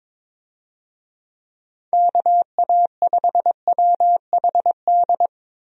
Morse code, heard as a keyed tone, about 22 wpm, 700 Hz. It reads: KA5WHD
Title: USA Call Signs - 6 Characters - 22wpm